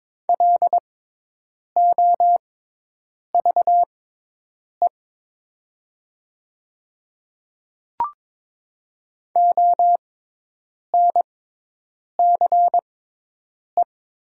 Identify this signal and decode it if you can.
Morse code, heard as a keyed tone, about 22 wpm, spaced as for 8 wpm, 700 Hz. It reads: LOVE ONCE
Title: 6x Character Spacing - Top 500 Words - 22wpm